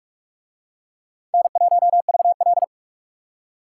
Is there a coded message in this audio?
Yes: Morse code from 1.3 to 2.6 s, heard as a keyed tone, about 40 wpm, 700 Hz. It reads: N1VL